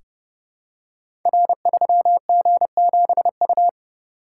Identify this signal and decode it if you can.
Morse code, heard as a keyed tone, about 30 wpm, 700 Hz. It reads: R3G7U
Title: Code Groups - 5 Letter:Numbers - 30wpm